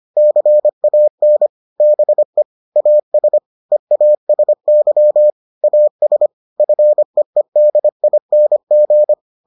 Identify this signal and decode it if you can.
Morse code, heard as a keyed tone, about 25 wpm, 600 Hz. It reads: CAN BE AS EASY AS FEEDING